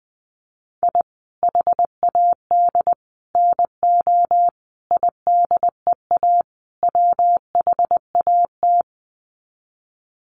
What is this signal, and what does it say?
Morse code, heard as a keyed tone, about 20 wpm, 700 Hz. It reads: I HAD NO IDEA WHAT
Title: Sets of 5 Words Encore 20wpm